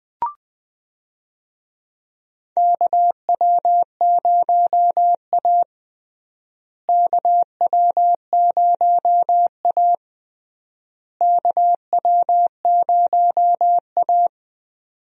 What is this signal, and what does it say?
Morse code, heard as a keyed tone, about 20 wpm, 700 Hz. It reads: KW0A KW0A KW0A
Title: USA Call Signs - 4 Characters - Repeated 3x - 20wpm